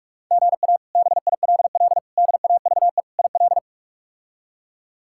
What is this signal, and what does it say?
Morse code, heard as a keyed tone, about 42 wpm, 700 Hz. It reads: GA BILL DAVE IL
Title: K1USN SST QSO Practice - 45wpm